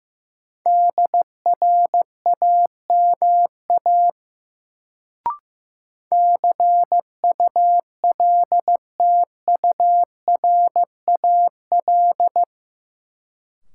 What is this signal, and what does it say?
Morse code, heard as a keyed tone, about 15 wpm, 700 Hz. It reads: DRAMA CULTURAL
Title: Sentences from Top 1500 Words - New Word Drill - 15wpm